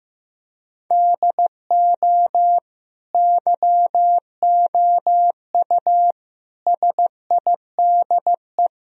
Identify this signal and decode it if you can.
Morse code, heard as a keyed tone, about 15 wpm, 700 Hz. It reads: DO YOU SIDE
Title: Sentences from Top 1000 Words - Review Part 1 - 15wpm